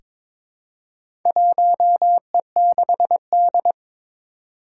Morse code, heard as a keyed tone, about 22 wpm, 700 Hz. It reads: 1E6D
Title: Four Challenging Letter-Number Combinations 22wpm